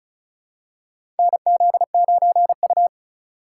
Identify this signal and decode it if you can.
Morse code, heard as a keyed tone, about 36 wpm, 690 Hz. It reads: NZ9U
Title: USA Call Signs - 4 Characters - 2x Character Spacing - 35wpm